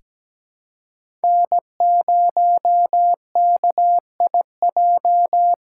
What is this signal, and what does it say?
Morse code, heard as a keyed tone, about 17 wpm, 700 Hz. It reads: N0KIJ
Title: USA Callsign DE Callsign - 5 Characters - 17wpm